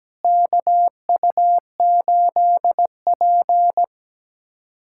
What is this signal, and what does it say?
Morse code, heard as a keyed tone, about 17 wpm, 700 Hz. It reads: KU8P
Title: USA Callsign DE Callsign - 4 Characters - 17wpm